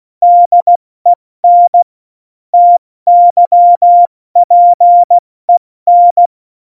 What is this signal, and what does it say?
Morse code, heard as a keyed tone, about 16 wpm, 700 Hz. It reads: DEN TYPEN
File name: Morsekurs / Witze / Witz 29. Mai 2018, 70 F80